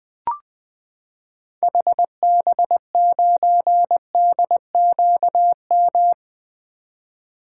Morse code, heard as a keyed tone, about 20 wpm, 700 Hz. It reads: HB9DQM
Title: 2023 SOTA Activator Call Signs - 20wpm